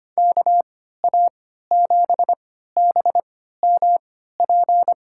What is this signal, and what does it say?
Morse code, heard as a keyed tone, about 25 wpm, spaced as for 14 wpm, 700 Hz. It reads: KA7BMP
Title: USA Call Signs - 6 Characters - 3x Character Spacing - 25wpm